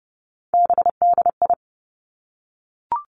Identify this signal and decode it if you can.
Morse code, heard as a keyed tone, about 30 wpm, 700 Hz. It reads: BDI
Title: Three Challenging Letter-Number Combinations 30wpm